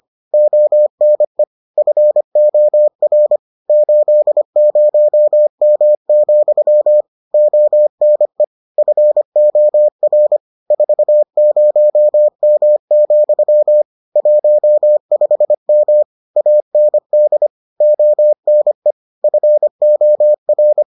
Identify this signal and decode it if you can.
Morse code, heard as a keyed tone, about 25 wpm, 600 Hz. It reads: ONE FOR 80M, ONE FOR 40M, 15M AND ONE FOR